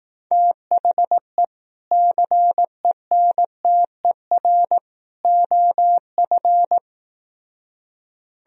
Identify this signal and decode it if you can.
Morse code, heard as a keyed tone, about 18 wpm, 700 Hz. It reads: THE CENTER OF
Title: Speed-Racing - Sets of 3 Words 15wpm